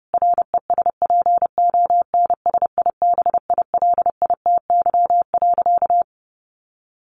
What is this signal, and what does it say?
Morse code, heard as a keyed tone, about 28 wpm, 700 Hz. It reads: RESPONSIBILITY.
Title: Sentences from Top 1500 Words 30wpm